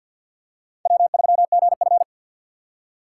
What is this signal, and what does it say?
Morse code, heard as a keyed tone, about 52 wpm, 700 Hz. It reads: W3GF
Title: USA Call Signs - 6x Character Spacing - 50wpm